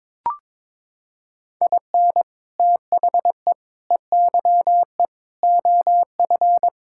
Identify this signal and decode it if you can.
Morse code, heard as a keyed tone, about 22 wpm, 700 Hz. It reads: IN THE EYE OF
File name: practice / 22wpm / Sets of 6 Words - 22wpm